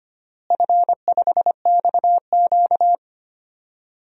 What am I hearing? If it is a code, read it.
Morse code, heard as a keyed tone, about 25 wpm, 700 Hz. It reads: F5XQ
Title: DXCC-100 Call Signs - 4 Characters - Repeated 3x - 25wpm